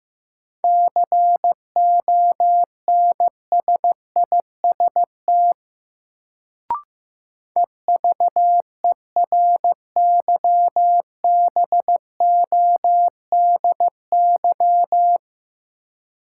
Morse code, heard as a keyed tone, about 15 wpm, 700 Hz. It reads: CONSIST EVERYBODY
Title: Sentences from Top 1400 Words - New Word Drill Alt - 15wpm